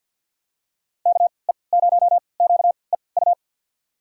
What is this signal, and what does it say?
Morse code, heard as a keyed tone, about 48 wpm, spaced as for 28 wpm, 700 Hz. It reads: KE0QEU
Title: USA Call Signs - 3x Character Spacing - 50wpm